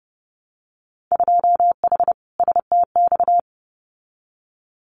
Morse code, heard as a keyed tone, about 30 wpm, 700 Hz. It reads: 2H STX
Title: Winter Field Day Exchanges Encore 30wpm